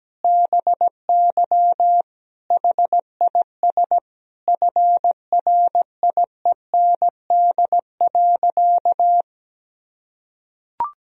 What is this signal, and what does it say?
Morse code, heard as a keyed tone, about 17 wpm, 700 Hz. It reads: BY HIS FRIEND.
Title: Sentences from Top 1000 Words - Review Part 4 - 17wpm